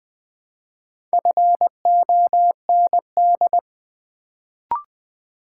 Morse code, heard as a keyed tone, about 20 wpm, 700 Hz. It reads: FOND
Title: Build A Word - 1 Letter At A Time - 4-Letter Words - 20wpm